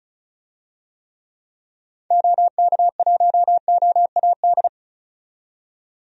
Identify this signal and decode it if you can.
Morse code, heard as a keyed tone, about 34 wpm, 700 Hz. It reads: OK1OAD
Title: DXCC-100 Call Signs - 6 Characters - 35wpm